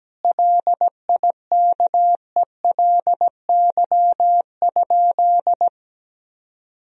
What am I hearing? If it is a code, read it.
Morse code, heard as a keyed tone, about 17 wpm, 700 Hz. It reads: LIKELY?
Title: Sentences from Top 1000 Words - Review Part 4 - 17wpm